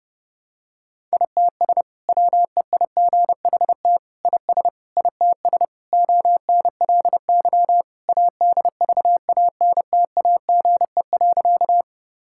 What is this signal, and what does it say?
Morse code, heard as a keyed tone, about 30 wpm, 700 Hz. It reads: ITS WEIGHT IS ITS ONLY ADVANTAGE.
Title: Sentences from Top 1000 Words - Review Part 6 - 30wpm